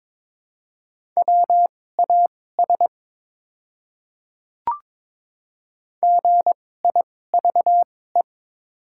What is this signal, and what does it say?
Morse code, heard as a keyed tone, about 22 wpm, spaced as for 16 wpm, 700 Hz. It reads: WAS GIVE
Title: 2x Character Spacing - Top 200 Words - 22wpm